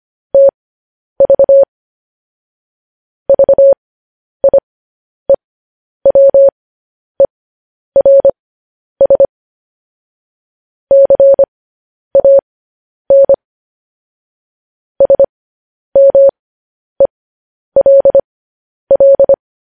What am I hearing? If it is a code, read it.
Morse code, heard as a keyed tone, about 25 wpm, spaced as for 10 wpm, 550 Hz. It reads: TV VIEWERS CAN SMELL